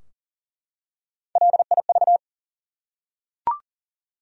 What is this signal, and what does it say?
Morse code, heard as a keyed tone, about 40 wpm, 700 Hz. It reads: LIV
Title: USA Common Names - Max 3 Letters - 40wpm